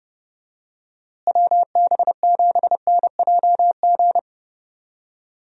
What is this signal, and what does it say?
Morse code, heard as a keyed tone, about 30 wpm, 700 Hz. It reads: WB7NJG
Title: USA Call Signs - 6 Characters - Repeated 3x - 30wpm